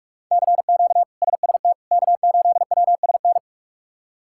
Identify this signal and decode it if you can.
Morse code, heard as a keyed tone, about 46 wpm, 700 Hz. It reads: CQ SST K8WSN
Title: K1USN SST QSO Practice - 45wpm